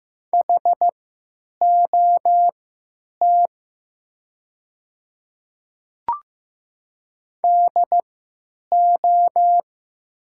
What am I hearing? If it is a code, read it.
Morse code, heard as a keyed tone, about 15 wpm, spaced as for 9 wpm, 700 Hz. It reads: HOT DO
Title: 3x Character Spacing - Top 100 Words - 15wpm